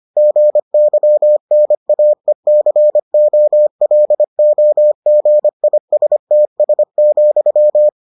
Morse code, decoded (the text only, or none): GYNAECOLOGISTS,